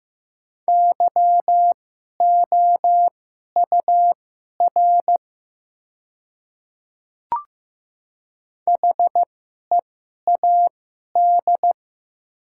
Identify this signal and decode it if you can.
Morse code, heard as a keyed tone, about 15 wpm, spaced as for 11 wpm, 700 Hz. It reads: YOUR HEAD
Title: 2x Character Spacing - Top 300 Words - 15wpm